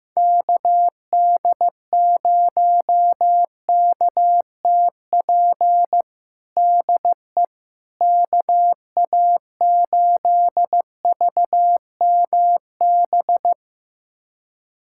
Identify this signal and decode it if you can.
Morse code, heard as a keyed tone, about 15 wpm, 700 Hz. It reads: KD0KTP DE KA8VMB
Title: USA Callsign DE Callsign 15wpm